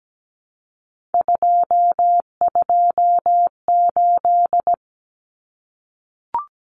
Morse code, heard as a keyed tone, about 17 wpm, 700 Hz. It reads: 228
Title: Common Cut Numbers - 3-Digit 17wpm